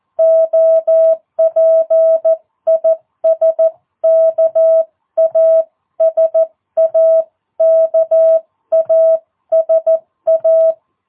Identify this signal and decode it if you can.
Morse code, heard as a keyed tone, about 14 wpm, 650 Hz. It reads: OP IS KASA KASA
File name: jh8cbh / 2022cw